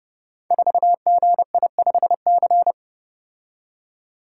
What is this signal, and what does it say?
Morse code, heard as a keyed tone, about 30 wpm, 700 Hz. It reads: 4GI5C
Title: Code Groups - 5 Letter:Numbers - 30wpm